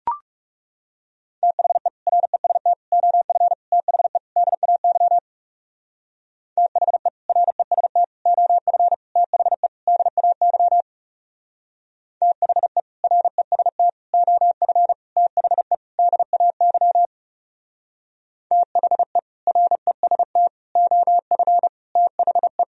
Morse code, heard as a keyed tone, about 36 wpm, 700 Hz. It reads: THE REST OF THE DAY THE REST OF THE DAY THE REST OF THE DAY THE REST OF THE
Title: Speed-Racing - Sets of 5 Words 30wpm